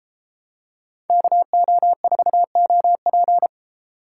Morse code, heard as a keyed tone, about 31 wpm, 700 Hz. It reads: KO4OP